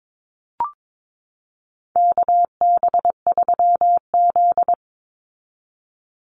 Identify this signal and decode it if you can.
Morse code, heard as a keyed tone, about 22 wpm, 700 Hz. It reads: KB3Z
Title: Call Signs Repeated 2x - 22wpm